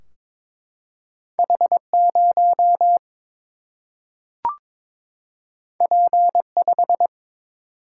Morse code, heard as a keyed tone, about 22 wpm, 700 Hz. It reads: H0 P5
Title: Single-Letter Single-Number 22wpm